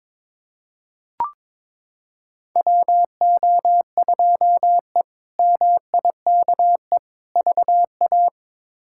Morse code, heard as a keyed tone, about 22 wpm, 700 Hz. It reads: WO2E MIKE VA
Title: K1USN SST QSO Practice - 22wpm